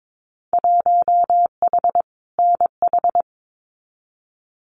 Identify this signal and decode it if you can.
Morse code, heard as a keyed tone, about 22 wpm, 700 Hz. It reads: 1H NH
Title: Winter Field Day Exchanges 22wpm